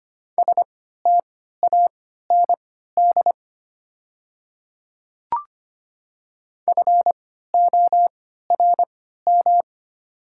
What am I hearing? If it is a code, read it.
Morse code, heard as a keyed tone, about 25 wpm, spaced as for 14 wpm, 690 Hz. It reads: STAND FORM
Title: 3x Character Spacing - Top 500 Words - 25wpm